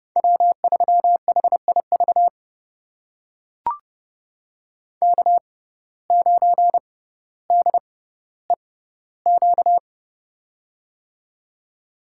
Morse code, heard as a keyed tone, about 30 wpm, spaced as for 15 wpm, 700 Hz. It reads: W3HIV K9DEQ
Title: USA Call Signs - 5 Characters - 6x Character Spacing - 30wpm